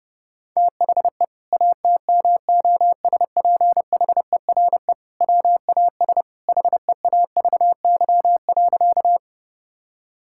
Morse code, heard as a keyed tone, about 32 wpm, 700 Hz. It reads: THE ATMOSPHERE WAS HEAVY.
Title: Sentences from Top 1600 Words 30wpm